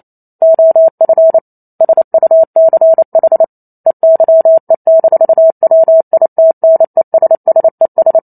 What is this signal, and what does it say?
Morse code, heard as a keyed tone, about 28 wpm, 650 Hz. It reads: OF SUCH EYE-WITNESSES